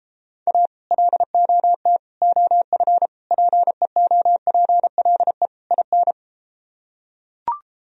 Morse code, heard as a keyed tone, about 33 wpm, 700 Hz. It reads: A LOT OF PEOPLE IN